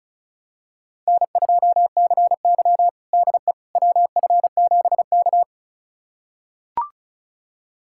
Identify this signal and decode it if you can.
Morse code, heard as a keyed tone, about 36 wpm, 700 Hz. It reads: N2CY DE WF7K
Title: USA Callsign DE Callsign - 4 Characters - 35wpm